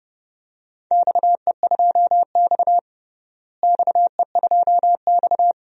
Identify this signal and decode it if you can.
Morse code, heard as a keyed tone, about 30 wpm, 700 Hz. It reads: XE2X XE2X
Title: DXCC-100 Call Signs - 4 Characters - Repeated 2x - 30wpm